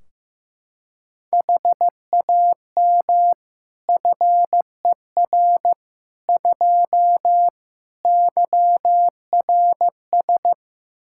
Morse code, heard as a keyed tone, about 15 wpm, 700 Hz. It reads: HAM FER 2 YRS
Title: Practice QSO - Exchange 3 - More Info - 15wpm